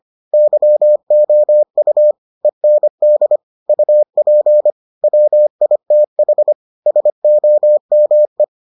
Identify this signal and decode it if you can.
Morse code, heard as a keyed tone, about 25 wpm, 600 Hz. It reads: YOU END UP WITH SOME